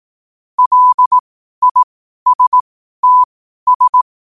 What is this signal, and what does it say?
Morse code, heard as a keyed tone, about 18 wpm, spaced as for 12 wpm, 1000 Hz. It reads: LISTS
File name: qotdfiles / 12wpm